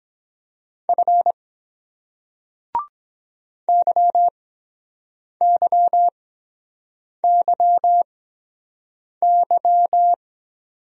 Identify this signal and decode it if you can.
Morse code, heard as a keyed tone, about 23 wpm, spaced as for 7 wpm, 700 Hz. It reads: F YYYY